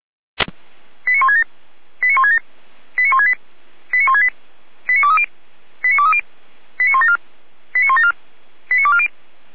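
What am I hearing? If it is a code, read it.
Five-tone selective call (ZVEI), heard as quick runs of five tones, tones of about 70 ms. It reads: 89167 89167 89168 89168 89230 89230 89174 89174 89240
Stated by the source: Five-tone ZVEI selective call received on an SDR.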